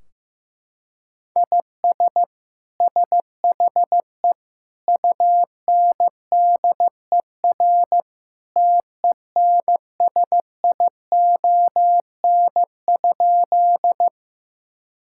Morse code, heard as a keyed tone, about 16 wpm, 700 Hz. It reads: IS SHE UNDER TENSION?
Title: Sentences from Top 1600 Words 15wpm